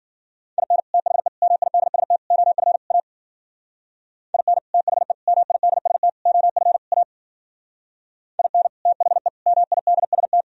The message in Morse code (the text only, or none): IN THE MIDST OF A IN THE MIDST OF A IN THE MIDST